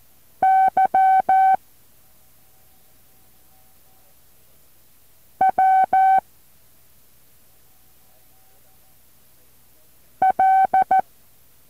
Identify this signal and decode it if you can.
Morse code, heard as a keyed tone, about 14 wpm, 760 Hz. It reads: Y W L
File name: intl morse code2